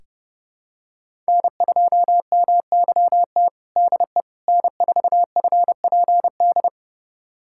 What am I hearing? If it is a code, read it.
Morse code, heard as a keyed tone, about 30 wpm, 700 Hz. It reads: N2MYT DE N4FPD
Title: USA Callsign DE Callsign - 5 Characters - 30wpm